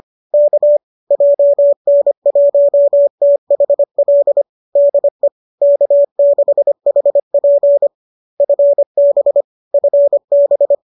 Morse code, heard as a keyed tone, about 25 wpm, 570 Hz. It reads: K JN1THL DE K6HP FB FB
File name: qso205 25